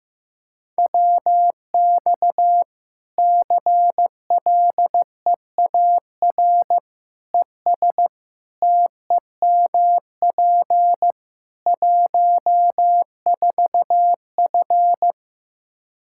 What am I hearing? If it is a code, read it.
Morse code, heard as a keyed tone, about 15 wpm, 700 Hz. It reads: WX CLEAR ES TEMP 14F
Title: K1USN SST QSO Practice 15wpm